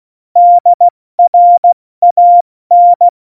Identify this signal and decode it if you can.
Morse code, heard as a keyed tone, about 16 wpm, 700 Hz. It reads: DRAN
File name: Morsekurs / Witze / Witz 26. Januar 2021, 70 F80